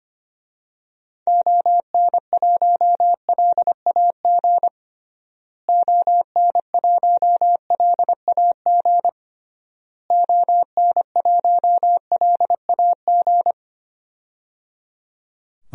Morse code, heard as a keyed tone, about 25 wpm, 700 Hz. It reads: ON1LAG ON1LAG ON1LAG
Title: DXCC-100 Call Signs - 6 Characters - Repeated 3x - 25wpm